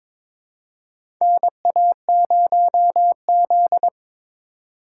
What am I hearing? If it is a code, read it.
Morse code, heard as a keyed tone, about 22 wpm, 700 Hz. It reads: NA0Z
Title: USA Call Signs - 4 Characters - 4x Character Spacing - 22wpm